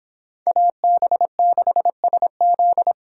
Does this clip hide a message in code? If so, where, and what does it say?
Yes: Morse code from 0.5 to 2.9 s, heard as a keyed tone, about 23 wpm, 700 Hz. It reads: AB6SZ